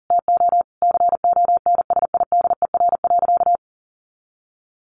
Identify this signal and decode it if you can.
Morse code, heard as a keyed tone, about 40 wpm, 700 Hz. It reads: TO CONSIDER.
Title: Sentences from Top 900 Words 40wpm